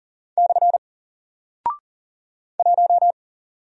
Morse code, heard as a keyed tone, about 38 wpm, 700 Hz. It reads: / 1
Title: Single Character (Letter:Number:Punctuation:Prosign) 40wpm